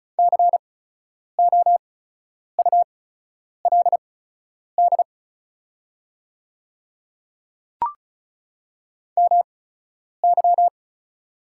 Morse code, heard as a keyed tone, about 36 wpm, spaced as for 10 wpm, 700 Hz. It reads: COULD MY
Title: 8x Character Spacing - Top 500 Words - 35wpm